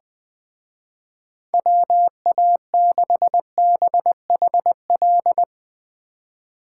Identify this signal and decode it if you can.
Morse code, heard as a keyed tone, about 22 wpm, 700 Hz. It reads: WA6BHL
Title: Speed Racing - USA 2x3 Call Signs 20wpm